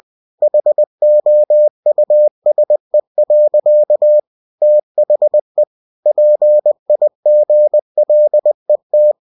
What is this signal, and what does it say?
Morse code, heard as a keyed tone, about 20 wpm, 600 Hz. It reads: HOUSE. THE PIGLET
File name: cw-books / winnie-the-pooh / winnie-20wpm-page-0078